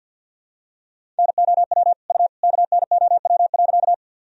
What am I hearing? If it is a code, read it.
Morse code, heard as a keyed tone, about 52 wpm, 700 Hz. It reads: NOW U KNOW.
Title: Sentences from Top 100 Words and Abbreviations 50wpm